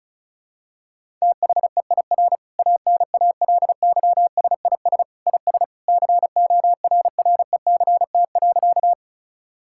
Morse code, heard as a keyed tone, about 36 wpm, 700 Hz. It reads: THEIR ANALYSIS IS CORRECT.
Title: Sentences from Top 1000 Words - Review Part 5 - 35wpm